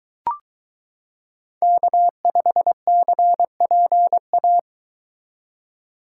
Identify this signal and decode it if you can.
Morse code, heard as a keyed tone, about 22 wpm, 700 Hz. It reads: K5CPA